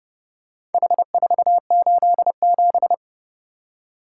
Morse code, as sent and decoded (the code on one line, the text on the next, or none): .... ....- ---.. --...
H487